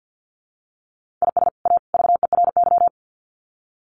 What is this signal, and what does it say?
Morse code, heard as a keyed tone, about 48 wpm, 700 Hz. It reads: IS A VERY
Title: Sets of 3 Words 50wpm